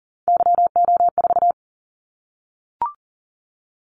Morse code, heard as a keyed tone, about 40 wpm, 700 Hz. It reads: YO4
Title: Callsign Prefixes for Top 25 DXCC - 40wpm